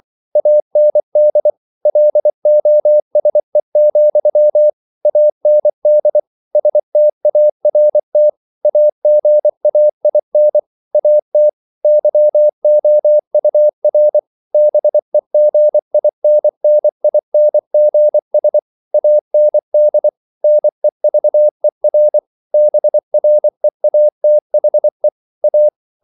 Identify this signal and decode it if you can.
Morse code, heard as a keyed tone, about 24 wpm, 600 Hz. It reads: AND LOSE, AND START AGAIN AT YOUR BEGINNINGS AND NEVER BREATHE A
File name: CW / Practice / If24